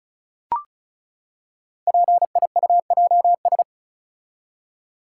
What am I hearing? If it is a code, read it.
Morse code, heard as a keyed tone, about 36 wpm, 690 Hz. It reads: PIUJS
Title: Code Groups - 5 Letters - 35wpm